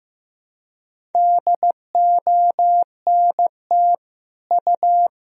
Morse code, heard as a keyed tone, about 15 wpm, 700 Hz. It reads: DONT U
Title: Sentences from Top 300 Words and Abbreviations 15wpm